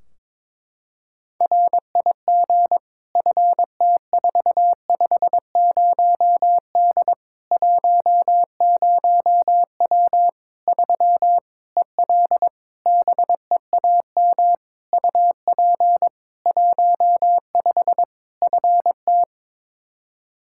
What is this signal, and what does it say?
Morse code, heard as a keyed tone, about 22 wpm, 700 Hz. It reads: RIG FT450D 10W 3 EL BEAM UP 15 FT
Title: Practice QSO - Exchange 2 - Info - 22wpm